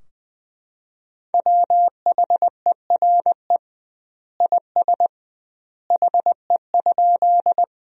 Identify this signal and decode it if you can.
Morse code, heard as a keyed tone, about 20 wpm, 700 Hz. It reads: WHERE IS HE?
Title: 2x Word-Spacing - Sentences from Top 200 Words - 20wpm